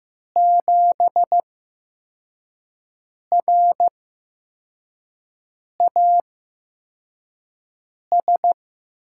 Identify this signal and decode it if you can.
Morse code, heard as a keyed tone, about 15 wpm, spaced as for 4 wpm, 700 Hz. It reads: 7RAS
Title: USA Call Signs - 8x Character Spacing - 15wpm